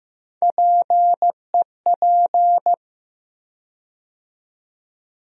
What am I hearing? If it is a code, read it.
Morse code, heard as a keyed tone, about 15 wpm, 700 Hz. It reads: PEP
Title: USA Call Sign Postfixes - 3 Letters 15wpm